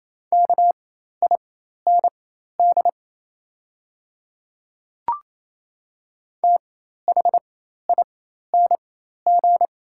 Morse code, heard as a keyed tone, about 28 wpm, spaced as for 13 wpm, 700 Hz. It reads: KIND THING
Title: 4x Character Spacing - Top 500 Words - 28wpm